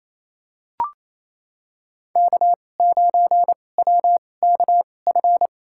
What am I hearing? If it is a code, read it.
Morse code, heard as a keyed tone, about 28 wpm, spaced as for 20 wpm, 700 Hz. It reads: K9WKF
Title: USA Call Signs - 5 Characters - 2x Character Spacing - 28wpm